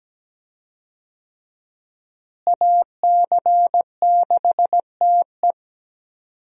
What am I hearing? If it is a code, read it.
Morse code, heard as a keyed tone, about 17 wpm, 700 Hz. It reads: AC6TE
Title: USA Call Signs - 5 Characters - Repeated 2x - 17wpm